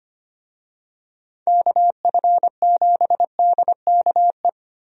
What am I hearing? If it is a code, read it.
Morse code, heard as a keyed tone, about 25 wpm, 700 Hz. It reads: KF7DKE
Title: USA Call Signs - 6 Characters - Repeated 3x - 25wpm